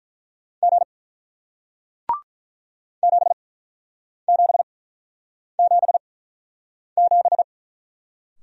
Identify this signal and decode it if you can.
Morse code, heard as a keyed tone, about 48 wpm, spaced as for 9 wpm, 700 Hz. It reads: G 7777